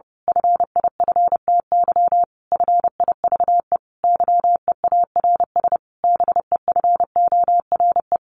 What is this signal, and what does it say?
Morse code, heard as a keyed tone, about 30 wpm, 700 Hz. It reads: FIFTY FIVE YEARS BEFORE
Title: A Sound Of Thunder 30wpm